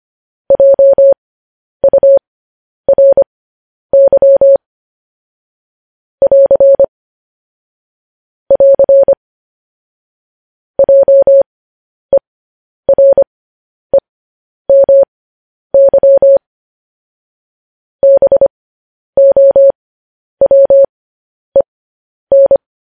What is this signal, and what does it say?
Morse code, heard as a keyed tone, about 25 wpm, spaced as for 10 wpm, 550 Hz. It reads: JURY <AR> <AR> JEREMY BOWEN